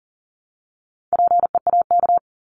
Morse code, heard as a keyed tone, about 40 wpm, 700 Hz. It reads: PEAK
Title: Top 1751-2000 Words 40wpm